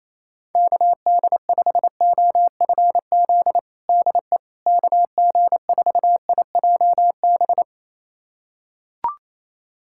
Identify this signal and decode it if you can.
Morse code, heard as a keyed tone, about 28 wpm, 700 Hz. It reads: KD5OFZ DE KG4IJB
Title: USA Callsign DE Callsign - 6 Characters - 28wpm